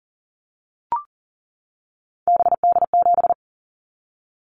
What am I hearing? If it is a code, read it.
Morse code, heard as a keyed tone, about 38 wpm, 700 Hz. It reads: BD7
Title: Callsign Prefixes for Top 100 DXCC - 40wpm